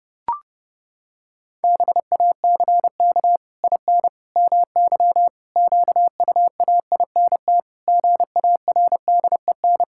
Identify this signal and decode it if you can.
Morse code, heard as a keyed tone, about 30 wpm, 700 Hz. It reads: BACK IN MY QUAINT GARDEN
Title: Pangram Challenge v2 - 30wpm